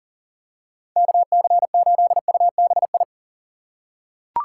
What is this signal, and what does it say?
Morse code, heard as a keyed tone, about 40 wpm, 700 Hz. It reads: KC8UBI
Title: USA Call Signs - 6 Characters - 4x Character Spacing - 40wpm